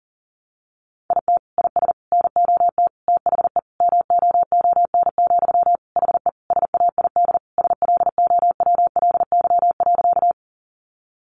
Morse code, heard as a keyed tone, about 40 wpm, 700 Hz. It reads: IT IS NOT THE MOON, HE SAID SLOWLY.
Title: Sentences from Top 1100 Words 40wpm